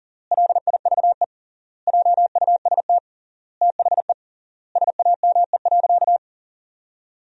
Morse code, heard as a keyed tone, about 40 wpm, 700 Hz. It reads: LIVE JUST THE SAME.
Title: 3x Word-Spacing - Sentences from Top 300 Words - 40wpm